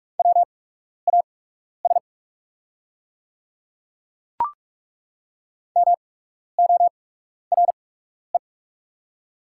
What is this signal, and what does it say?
Morse code, heard as a keyed tone, about 43 wpm, spaced as for 12 wpm, 700 Hz. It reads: WAS MORE
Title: 8x Character Spacing - Top 200 Words - 45wpm